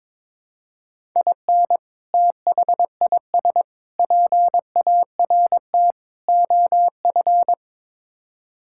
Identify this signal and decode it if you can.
Morse code, heard as a keyed tone, about 22 wpm, 700 Hz. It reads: IN THIS PART OF
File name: practice / 22wpm / Build A Phrase - 5-Word Phrases - 22wpm